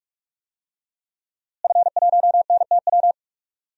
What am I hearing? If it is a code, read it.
Morse code, heard as a keyed tone, about 45 wpm, 700 Hz. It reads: F1NTW
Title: DXCC-100 Call Signs - 5 Characters - Repeated 2x - 45wpm